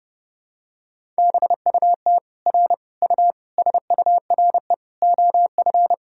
Morse code, heard as a keyed tone, about 32 wpm, 700 Hz. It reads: BUT R U SURE OF
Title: Sentences from Top 300 Words and Abbreviations 30wpm